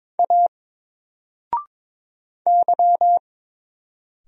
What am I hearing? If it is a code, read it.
Morse code, heard as a keyed tone, about 22 wpm, 700 Hz. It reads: A Y
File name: Single-Letter-Number-22wpm